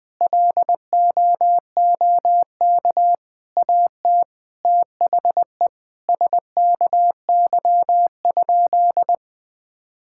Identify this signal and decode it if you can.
Morse code, heard as a keyed tone, about 20 wpm, 700 Hz. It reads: LOOK AT THE SKY?
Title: Sentences with 4-Word Phrases - 20wpm